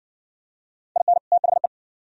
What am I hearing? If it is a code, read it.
Morse code, heard as a keyed tone, about 60 wpm, 700 Hz. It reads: IN THE